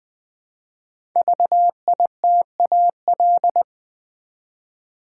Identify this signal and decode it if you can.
Morse code, heard as a keyed tone, about 20 wpm, 700 Hz. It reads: VITAL
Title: Build A Word - 1 Letter At A Time - 5-Letter Words - 20wpm